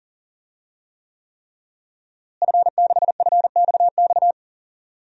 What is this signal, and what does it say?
Morse code, heard as a keyed tone, about 40 wpm, 700 Hz. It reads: F6FXX
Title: DXCC-100 Call Signs - 5 Characters - Repeated 3x - 40wpm